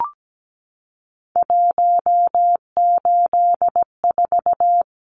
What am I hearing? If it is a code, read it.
Morse code, heard as a keyed tone, about 17 wpm, 700 Hz. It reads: 184
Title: Numbers - 3-Digit 17wpm